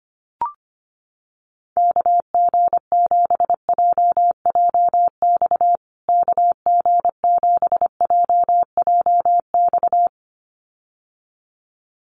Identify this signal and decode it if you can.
Morse code, heard as a keyed tone, about 25 wpm, 700 Hz. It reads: KG7JJX KG7JJX
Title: USA Call Signs Repeated 2x 25wpm